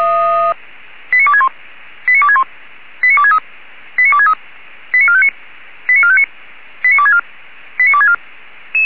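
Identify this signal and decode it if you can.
Five-tone selective call (ZVEI), heard as quick runs of five tones, tones of about 70 ms. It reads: 89261 89261 89263 89263 89459 89459 89264 89264
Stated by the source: ZVEI five-tone selective call received on an SDR.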